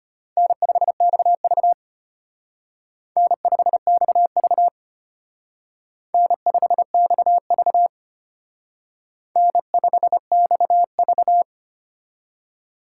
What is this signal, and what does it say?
Morse code, heard as a keyed tone, about 32 wpm, 700 Hz. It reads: N5XV N5XV N5XV N5XV